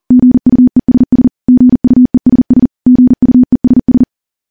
Morse code, heard as a keyed tone, about 40 wpm, 250 Hz. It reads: GUESS GUESS GUESS
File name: morse / morse-training-40WPM